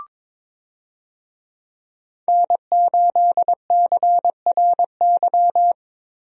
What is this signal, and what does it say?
Morse code, heard as a keyed tone, about 22 wpm, 700 Hz. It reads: N8CRY
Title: USA Call Signs - 5 Characters - 22wpm